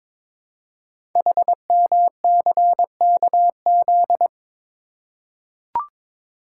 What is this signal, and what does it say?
Morse code, heard as a keyed tone, about 22 wpm, 700 Hz. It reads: HMCKZ
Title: Code Groups - 5 Letters - 22wpm